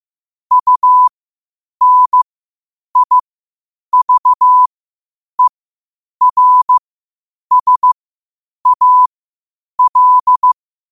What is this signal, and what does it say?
Morse code, heard as a keyed tone, about 15 wpm, spaced as for 8 wpm, 1000 Hz. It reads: UNIVERSAL
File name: qotdfiles / 8wpm